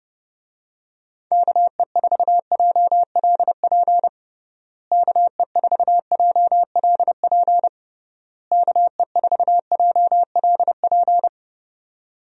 Morse code, heard as a keyed tone, about 29 wpm, 700 Hz. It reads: KE4JLP KE4JLP KE4JLP
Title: USA Call Signs - 6 Characters - Repeated 3x - 30wpm